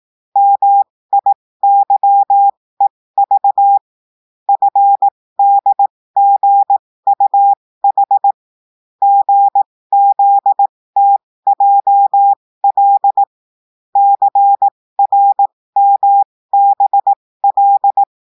Morse code, heard as a keyed tone, about 18 wpm, spaced as for 15 wpm, 800 Hz. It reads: MIYEV FDGUH GZTJL CRMBL
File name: cours-du-lundi files / cours-1580-texte-chiffre-15-mpm